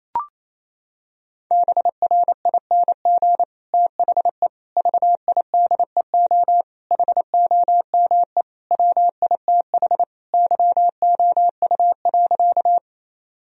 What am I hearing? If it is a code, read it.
Morse code, heard as a keyed tone, about 27 wpm, 700 Hz. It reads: BRING THE VIDEO HOME WITH YOU.
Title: Sentences from Top 1000 Words - Review Part 4 - 28wpm